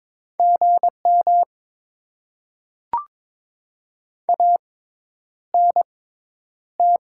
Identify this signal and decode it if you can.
Morse code, heard as a keyed tone, about 21 wpm, spaced as for 10 wpm, 700 Hz. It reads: GM ANT
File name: Top 20 QSO Elements - 6x Character Spacing - 22wpm